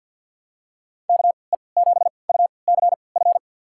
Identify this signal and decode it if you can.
Morse code, heard as a keyed tone, about 53 wpm, spaced as for 29 wpm, 700 Hz. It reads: KE7UCF
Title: USA Call Signs - 3x Character Spacing - 50wpm